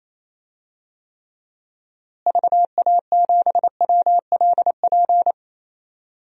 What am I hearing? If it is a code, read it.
Morse code, heard as a keyed tone, about 28 wpm, 700 Hz. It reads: VA7WLP
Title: DXCC-100 Call Signs - 6 Characters - Repeated 2x - 28wpm